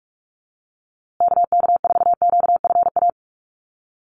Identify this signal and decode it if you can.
Morse code, heard as a keyed tone, about 44 wpm, 700 Hz. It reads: KK4QFA
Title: USA Call Signs Repeated 3x 45wpm